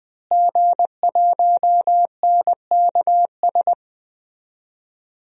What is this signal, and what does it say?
Morse code, heard as a keyed tone, about 21 wpm, 700 Hz. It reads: G1NKS
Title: DXCC-100 Call Signs - 5 Characters - Repeated 3x - 20wpm